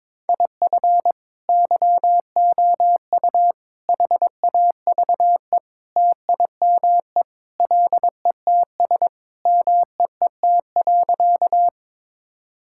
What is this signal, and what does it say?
Morse code, heard as a keyed tone, about 22 wpm, 700 Hz. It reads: IF YOU HAVE TIME LETS MEET.
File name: practice / 22wpm / Sentences with 3-Word Phrases - 22wpm